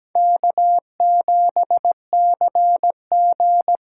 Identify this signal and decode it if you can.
Morse code, heard as a keyed tone, about 20 wpm, 700 Hz. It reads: K7CG
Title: Speed Racing - USA 1x2 Call Signs 17wpm